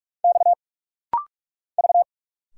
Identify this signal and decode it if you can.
Morse code, heard as a keyed tone, about 46 wpm, 700 Hz. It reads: X V
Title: Single Letters - Rapid-Fire 45wpm